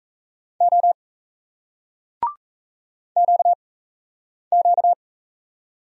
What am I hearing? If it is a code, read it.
Morse code, heard as a keyed tone, about 36 wpm, 700 Hz. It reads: O Q Q